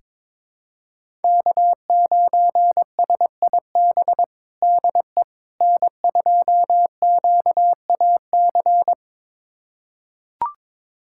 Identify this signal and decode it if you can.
Morse code, heard as a keyed tone, about 22 wpm, 700 Hz. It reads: K9SIB DE N2QAC
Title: USA Callsign DE Callsign - 5 Characters - 22wpm